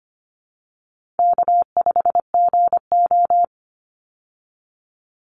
Morse code, heard as a keyed tone, about 25 wpm, 700 Hz. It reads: K5GO
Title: Easy Call Signs Repeated 2x - 25wpm